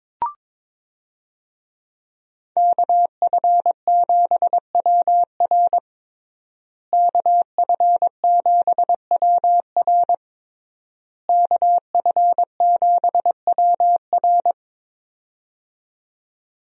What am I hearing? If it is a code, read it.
Morse code, heard as a keyed tone, about 21 wpm, 700 Hz. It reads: KF7WR KF7WR KF7WR
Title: USA Call Signs - 5 Characters - Repeated 3x - 22wpm